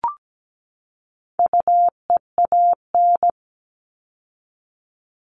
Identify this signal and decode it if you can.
Morse code, heard as a keyed tone, about 17 wpm, 700 Hz. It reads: UEAN
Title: All Cut Numbers - 4-Digit 17wpm